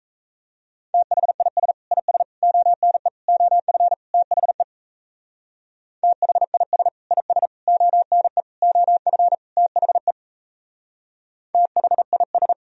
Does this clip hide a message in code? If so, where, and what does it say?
Yes: Morse code from 0.9 to 12.5 s, heard as a keyed tone, about 36 wpm, 700 Hz. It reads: THIS IS ONE OF THE THIS IS ONE OF THE THIS